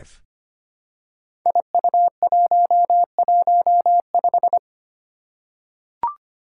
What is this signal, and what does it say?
Morse code, heard as a keyed tone, about 25 wpm, 700 Hz. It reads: IU115
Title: Five Challenging Letter-Number Combinations 25wpm